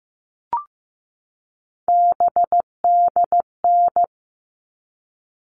Morse code, heard as a keyed tone, about 15 wpm, 700 Hz. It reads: BDN
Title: All Cut Numbers - 3-Digit 15wpm